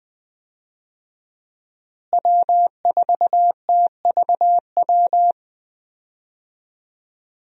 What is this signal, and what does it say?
Morse code, heard as a keyed tone, about 20 wpm, 700 Hz. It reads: W4TVW
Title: USA Call Signs - 5 Characters - Repeated 3x - 20wpm